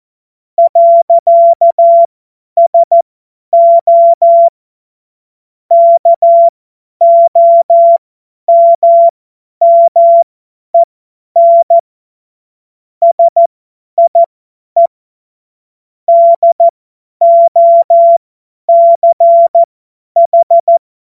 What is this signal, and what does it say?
Morse code, heard as a keyed tone, about 14 wpm, spaced as for 10 wpm, 680 Hz. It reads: . SO KOMMEN SIE DOCH